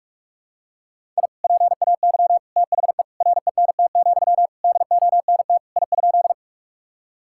Morse code, heard as a keyed tone, about 46 wpm, 700 Hz. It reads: I PAY THE RENT, DONT I?
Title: Sentences from Top 1000 Words - Review Part 5 - 45wpm